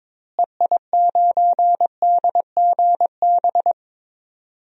Morse code, heard as a keyed tone, about 22 wpm, 700 Hz. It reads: EI9DGB
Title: DXCC-100 Call Signs - 6 Characters - Repeated 2x - 22wpm